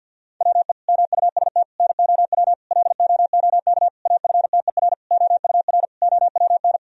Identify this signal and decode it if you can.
Morse code, heard as a keyed tone, about 50 wpm, 700 Hz. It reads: WE MUST NOW LOOK AFTER OUR OWN
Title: Sentences from Top 1000 Words - Review Part 4 - 50wpm